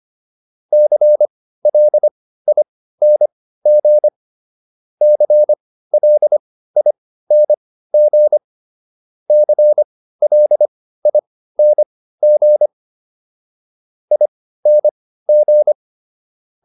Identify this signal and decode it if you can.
Morse code, heard as a keyed tone, about 25 wpm, spaced as for 15 wpm, 600 Hz. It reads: CLING CLING CLING ING